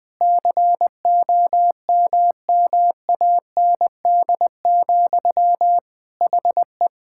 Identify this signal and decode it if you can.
Morse code, heard as a keyed tone, about 20 wpm, 700 Hz. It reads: COMMAND, HE
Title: Sentences from Top 1600 Words 20wpm